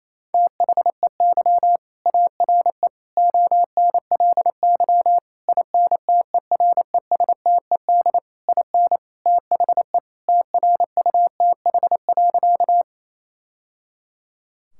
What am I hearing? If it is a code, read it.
Morse code, heard as a keyed tone, about 28 wpm, 700 Hz. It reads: THEY ARE ONLY INTERESTED IN THE TRUTH.
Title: Sentences from Top 1000 Words - Review Part 6 - 28wpm